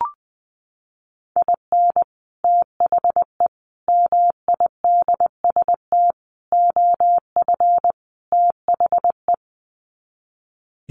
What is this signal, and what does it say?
Morse code, heard as a keyed tone, about 20 wpm, 700 Hz. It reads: IN THE MIDST OF THE
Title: Sets of 5 Words 20wpm